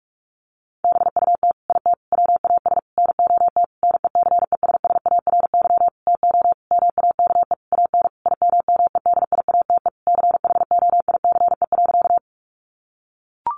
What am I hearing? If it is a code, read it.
Morse code, heard as a keyed tone, about 44 wpm, 700 Hz. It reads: BUT IT WAS NOT NECESSARY TO MAKE AN IMMEDIATE CHOICE.
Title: Sentences from Top 1200 Words 45wpm